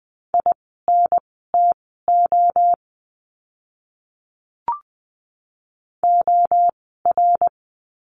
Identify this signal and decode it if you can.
Morse code, heard as a keyed tone, about 20 wpm, spaced as for 14 wpm, 700 Hz. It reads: INTO OR
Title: 108 Lesson 18f - Letter m - Words 20wpm